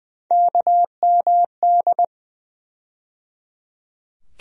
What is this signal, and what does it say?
Morse code, heard as a keyed tone, about 20 wpm, 700 Hz. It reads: KMD
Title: USA Call Sign Postfixes - 3 Letters 20wpm